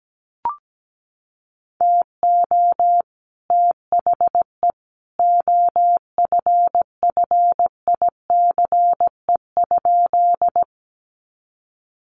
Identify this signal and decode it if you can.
Morse code, heard as a keyed tone, about 17 wpm, 700 Hz. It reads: TO THE OFFICE?
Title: Sentences from Top 1000 Words 17wpm